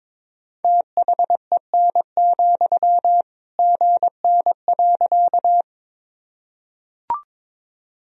Morse code, heard as a keyed tone, about 22 wpm, 700 Hz. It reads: THEN, GN.
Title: Sentences from Top 300 Words and Abbreviations 22wpm